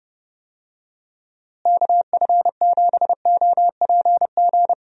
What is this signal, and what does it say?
Morse code, heard as a keyed tone, about 30 wpm, 700 Hz. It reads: KF7OPG
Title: USA Call Signs - 6 Characters - Repeated 3x - 30wpm